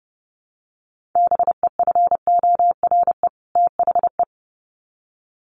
Morse code, heard as a keyed tone, about 30 wpm, 700 Hz. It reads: BEFORE THE
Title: Sets of 2 Words 30wpm